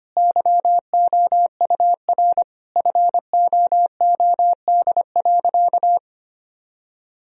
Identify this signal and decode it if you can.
Morse code, heard as a keyed tone, about 25 wpm, 700 Hz. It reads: YOUR FOOD.
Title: Sentences from Top 1000 Words - Review Part 2 - 25wpm